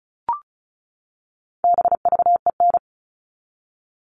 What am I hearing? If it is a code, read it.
Morse code, heard as a keyed tone, about 34 wpm, 700 Hz. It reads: BVEN
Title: All Cut Numbers - 4-Digit 35wpm